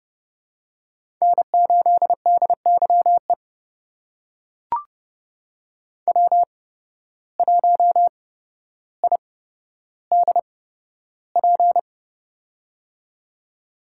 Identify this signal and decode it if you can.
Morse code, heard as a keyed tone, about 30 wpm, spaced as for 13 wpm, 700 Hz. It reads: N8DYE W1IDP
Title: USA Call Signs - 5 Characters - 8x Character Spacing - 30wpm